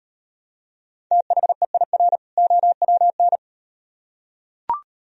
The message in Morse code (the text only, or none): THEIR OWN